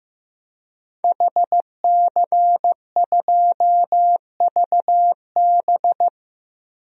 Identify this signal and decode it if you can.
Morse code, heard as a keyed tone, about 15 wpm, 700 Hz. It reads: HC2VB
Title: DXCC-100 Call Signs - 5 Characters - Repeated 2x - 15wpm